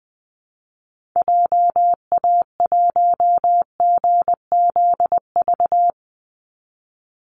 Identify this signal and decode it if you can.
Morse code, heard as a keyed tone, about 20 wpm, 700 Hz. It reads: JA1GZV
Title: Call Signs Repeated 2x - 20wpm